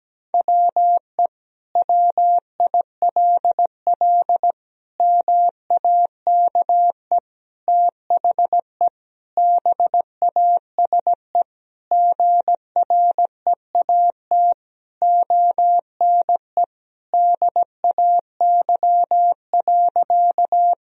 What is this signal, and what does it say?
Morse code, heard as a keyed tone, about 17 wpm, 700 Hz. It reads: WE WILL MAKE THE BASE GREAT ONE DAY.
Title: Sentences from Top 1000 Words - Review Part 1 - 17wpm